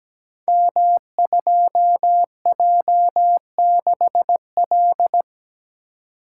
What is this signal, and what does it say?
Morse code, heard as a keyed tone, about 17 wpm, 700 Hz. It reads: M2J6L
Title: Code Groups - 5 Letter:Numbers - 17wpm